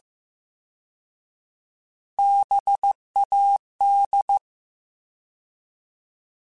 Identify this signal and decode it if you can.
Morse code, heard as a keyed tone, about 15 wpm, 800 Hz. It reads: BAD
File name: Alphabetical / Alphabetical mp3s / Bs @13